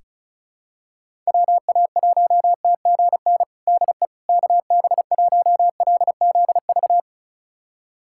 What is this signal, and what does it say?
Morse code, heard as a keyed tone, about 36 wpm, 700 Hz. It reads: WA1TGN DE KB1LZV
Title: USA Callsign DE Callsign - 6 Characters - 35wpm